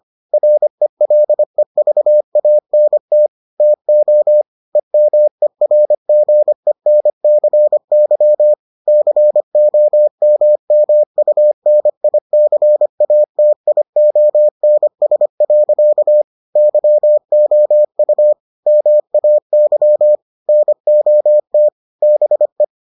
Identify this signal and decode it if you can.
Morse code, heard as a keyed tone, about 25 wpm, 600 Hz. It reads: RELEVANT TO EMERGENCY COMMUNICATIONS. YOU MAY NOT BE